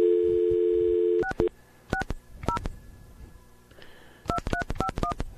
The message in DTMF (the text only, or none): 660 2351